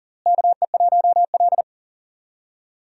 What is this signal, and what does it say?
Morse code, heard as a keyed tone, about 38 wpm, 700 Hz. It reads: KE1L
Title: USA Call Signs - 4 Characters - Repeated 2x - 40wpm